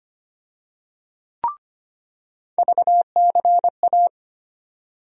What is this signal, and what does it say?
Morse code, heard as a keyed tone, about 25 wpm, 700 Hz. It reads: VCA
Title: USA Call Sign Postfixes - 3 Letters 25wpm